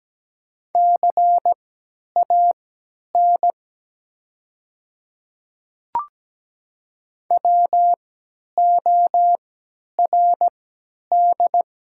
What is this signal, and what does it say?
Morse code, heard as a keyed tone, about 17 wpm, spaced as for 10 wpm, 700 Hz. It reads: CAN WORD
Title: 3x Character Spacing - Top 100 Words - 17wpm